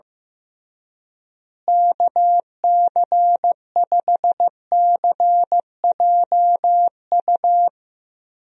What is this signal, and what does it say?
Morse code, heard as a keyed tone, about 15 wpm, 700 Hz. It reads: KC5CJU
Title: USA Call Signs - 6 Characters - Repeated 3x - 15wpm